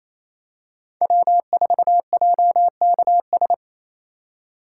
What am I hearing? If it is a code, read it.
Morse code, heard as a keyed tone, about 28 wpm, 700 Hz. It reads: W4JKS
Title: USA Call Signs - 3x Character Spacing - 28wpm